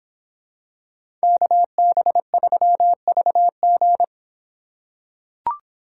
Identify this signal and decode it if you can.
Morse code, heard as a keyed tone, about 25 wpm, 700 Hz. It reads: KB3VG